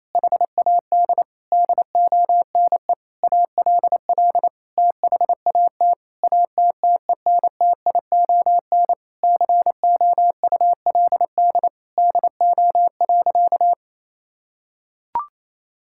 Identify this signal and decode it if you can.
Morse code, heard as a keyed tone, about 28 wpm, 700 Hz. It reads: HAD DONE ALL THAT ATTENTION COULD DO.
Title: Sentences from Top 1000 Words - Review Part 6 - 28wpm